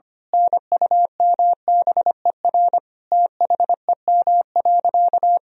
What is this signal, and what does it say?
Morse code, heard as a keyed tone, about 25 wpm, 700 Hz. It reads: NUMBER THEM.
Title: Sentences from Top 1000 Words - Review Part 1 - 25wpm